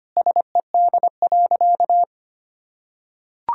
Morse code, heard as a keyed tone, about 25 wpm, 700 Hz. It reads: SED.
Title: Sentences from Top 100 Words and Abbreviations 25wpm